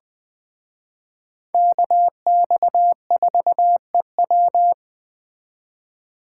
Morse code, heard as a keyed tone, about 20 wpm, 700 Hz. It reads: KX4EW
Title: Speed Racing - USA 2x2 Call Signs 17wpm